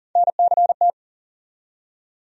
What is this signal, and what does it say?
Morse code, heard as a keyed tone, about 40 wpm, 700 Hz. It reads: NCT
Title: USA Call Sign Postfixes - 3 Letters 40wpm